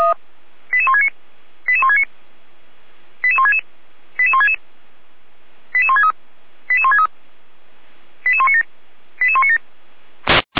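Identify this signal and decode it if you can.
Five-tone selective call (ZVEI), heard as quick runs of five tones, tones of about 70 ms. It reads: 88169 88169 88166 88166 88163 88163 88197 88197
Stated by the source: ZVEI five-tone selective call 88352 received on an SDR.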